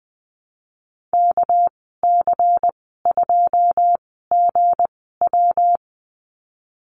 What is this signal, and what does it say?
Morse code, heard as a keyed tone, about 20 wpm, spaced as for 14 wpm, 700 Hz. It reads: KC2GW
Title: 213 Lesson 34g - Number 8 - Call Signs 20wpm